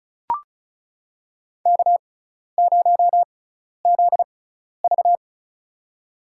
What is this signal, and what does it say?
Morse code, heard as a keyed tone, about 34 wpm, spaced as for 12 wpm, 700 Hz. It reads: K0ZV
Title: USA Call Signs - 4 Characters - 6x Character Spacing - 35wpm